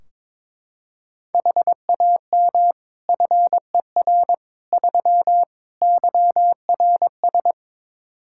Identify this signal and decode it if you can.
Morse code, heard as a keyed tone, about 23 wpm, 700 Hz. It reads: HAM FER 3 YRS
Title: Practice QSO - Exchange 3 - More Info - 22wpm